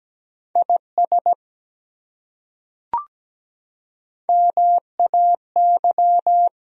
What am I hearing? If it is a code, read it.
Morse code, heard as a keyed tone, about 17 wpm, 700 Hz. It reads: IS MAY
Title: Top 100 Words Repeated 17wpm